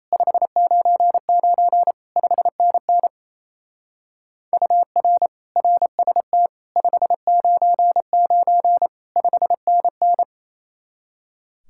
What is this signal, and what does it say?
Morse code, heard as a keyed tone, about 35 wpm, 700 Hz. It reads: 599 5NN UR RST 599 5NN
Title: Speed-Racing Common QSO Phrases Encore - 28wpm